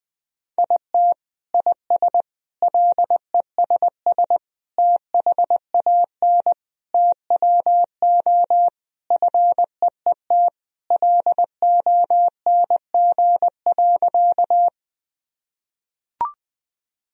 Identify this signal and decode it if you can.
Morse code, heard as a keyed tone, about 21 wpm, 700 Hz. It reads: IT IS LESS THAN TWO FEET LONG.
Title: Sentences from Top 1000 Words - Review Part 2 - 20wpm